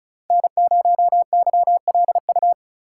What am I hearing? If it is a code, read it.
Morse code, heard as a keyed tone, about 36 wpm, 700 Hz. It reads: N0YLU
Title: USA Call Signs - 5 Characters - 2x Character Spacing - 35wpm